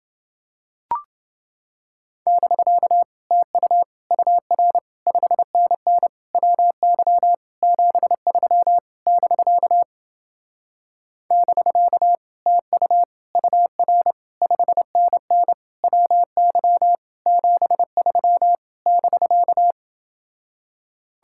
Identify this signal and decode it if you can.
Morse code, heard as a keyed tone, about 30 wpm, 700 Hz. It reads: <BK> TU UR 5NN WY 73 <BK> <BK> TU UR 5NN WY 73 <BK>